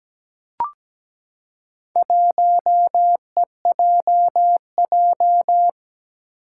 Morse code, heard as a keyed tone, about 17 wpm, 700 Hz. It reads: 1EJJ
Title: Four Challenging Letter-Number Combinations 17wpm